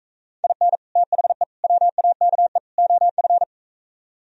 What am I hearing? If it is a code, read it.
Morse code, heard as a keyed tone, about 45 wpm, 700 Hz. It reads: IN THE WAKE OF